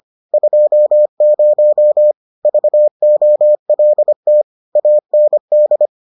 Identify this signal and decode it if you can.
Morse code, heard as a keyed tone, about 25 wpm, 600 Hz. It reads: 20 VOLT AND